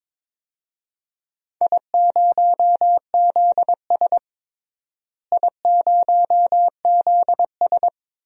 Morse code, heard as a keyed tone, about 22 wpm, 700 Hz. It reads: I0ZS I0ZS
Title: DXCC-100 Call Signs - 4 Characters - Repeated 3x - 22wpm